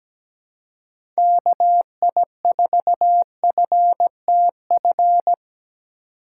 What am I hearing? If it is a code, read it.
Morse code, heard as a keyed tone, about 17 wpm, 700 Hz. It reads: KI4FTF
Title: USA Call Signs - 6 Characters - 4x Character Spacing - 17wpm